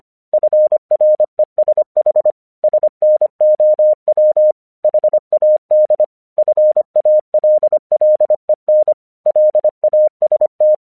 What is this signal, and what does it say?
Morse code, heard as a keyed tone, about 25 wpm, 610 Hz. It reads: FRESH SNOW HAD FALLEN LAST
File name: English 25wpm